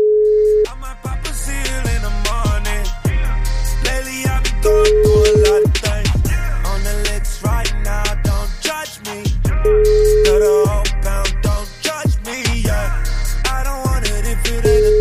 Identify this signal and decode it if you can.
Ringback tone (Europe, 425 Hz), with noise.